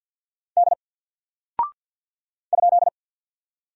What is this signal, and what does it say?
Morse code, heard as a keyed tone, about 48 wpm, 700 Hz. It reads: D ?
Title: Single Character (Letter:Number:Punctuation:Prosign) Alt 50wpm